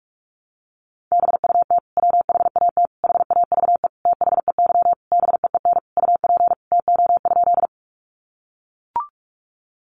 Morse code, heard as a keyed tone, about 44 wpm, 700 Hz. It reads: BUT WHAT HAVE THEY BEEN UP TO?
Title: Sentences from Top 100 Words 45wpm